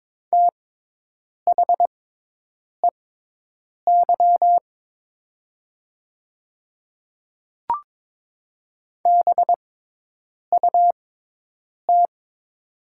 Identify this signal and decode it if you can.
Morse code, heard as a keyed tone, about 22 wpm, spaced as for 8 wpm, 700 Hz. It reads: THEY BUT
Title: 6x Character Spacing - Top 100 Words - 22wpm